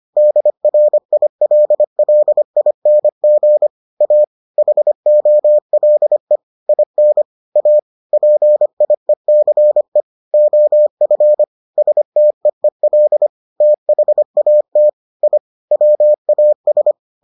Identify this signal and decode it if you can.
Morse code, heard as a keyed tone, about 25 wpm, 600 Hz. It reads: DRILLING A HOLE IN A PIECE OF STEEL THAT I WAS